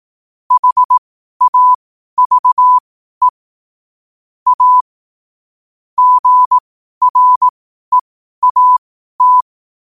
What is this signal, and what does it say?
Morse code, heard as a keyed tone, about 18 wpm, spaced as for 12 wpm, 1000 Hz. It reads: HAVE A GREAT
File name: qotdfiles / 12wpm